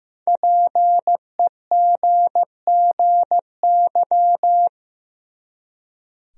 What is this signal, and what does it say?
Morse code, heard as a keyed tone, about 15 wpm, 700 Hz. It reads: PEGGY
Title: USA Common Names - Max 5 Letters - 15wpm